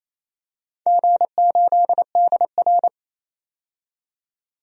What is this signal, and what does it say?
Morse code, heard as a keyed tone, about 28 wpm, 700 Hz. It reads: G8DR
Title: DXCC-100 Call Signs - 4 Characters - 28wpm